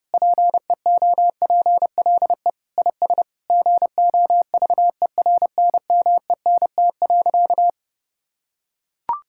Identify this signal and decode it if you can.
Morse code, heard as a keyed tone, about 30 wpm, 700 Hz. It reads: PEOPLE IS GOVERNMENT.
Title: Sentences from Top 1000 Words - Review Part 6 - 30wpm